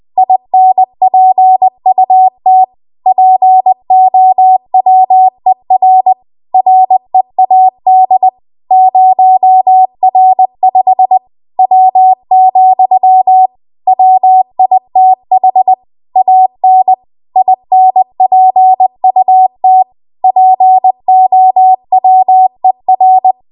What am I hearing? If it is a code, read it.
Morse code, heard as a keyed tone, about 20 wpm, 750 Hz. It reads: INPUT POWER READ 0R5 W, WITH AN INPUT POWER
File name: Archive / 20 WPM / 20WPM